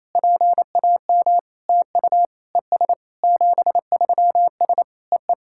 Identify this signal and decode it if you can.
Morse code, heard as a keyed tone, about 28 wpm, 700 Hz. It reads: PAM TU ES 73S EE
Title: SOTA QSO Practice 28wpm